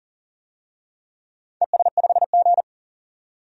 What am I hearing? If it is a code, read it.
Morse code, heard as a keyed tone, about 38 wpm, 700 Hz. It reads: ES5G
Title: DXCC-100 Call Signs - 4 Characters - 40wpm